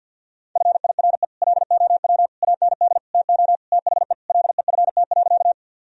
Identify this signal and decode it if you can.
Morse code, heard as a keyed tone, about 52 wpm, 700 Hz. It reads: FIRE LOW AND TO THE LEFT.
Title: Sentences from Top 1000 Words - Review Part 1 - 50wpm